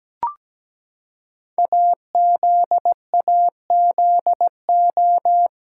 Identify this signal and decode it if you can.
Morse code, heard as a keyed tone, about 17 wpm, 700 Hz. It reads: AZAZO
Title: Code Groups - 5 Letters - 17wpm